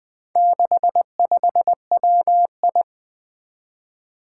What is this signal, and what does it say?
Morse code, heard as a keyed tone, about 20 wpm, 700 Hz. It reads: 65WI
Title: Four Challenging Letter-Number Combinations 20wpm